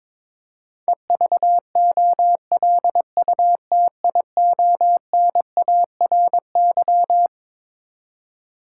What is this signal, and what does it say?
Morse code, heard as a keyed tone, about 22 wpm, 700 Hz. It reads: EVOLUTIONARY
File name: Twelve Letter Words - 22wpm